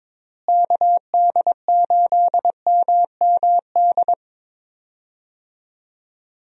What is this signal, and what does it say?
Morse code, heard as a keyed tone, about 22 wpm, 700 Hz. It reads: KD8MMD
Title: USA Call Signs - 6 Characters - Repeated 3x - 22wpm